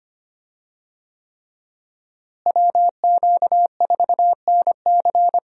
Morse code, heard as a keyed tone, about 25 wpm, 700 Hz. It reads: WQ4NC
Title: USA Call Signs - 5 Characters - Repeated 3x - 25wpm